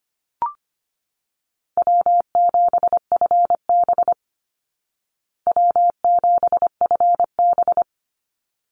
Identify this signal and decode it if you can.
Morse code, heard as a keyed tone, about 25 wpm, 700 Hz. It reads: W7FB W7FB
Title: Call Signs Repeated 3x - 25wpm